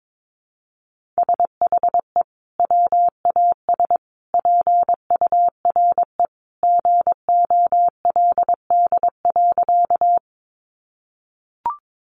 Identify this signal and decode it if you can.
Morse code, heard as a keyed tone, about 22 wpm, 700 Hz. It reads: SHE WAS PURE GOLD.
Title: Sentences from Top 1500 Words 22wpm